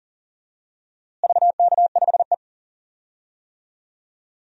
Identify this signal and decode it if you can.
Morse code, heard as a keyed tone, about 40 wpm, 700 Hz. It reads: VK5E
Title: DXCC-100 Call Signs - 4 Characters - Repeated 2x - 40wpm